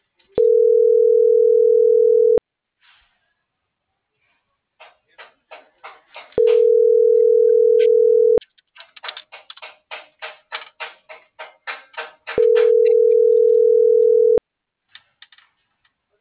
Ringback tone (North America), 440 + 480 Hz.
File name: CENTRAL MABE / out antonio.garcia - .96612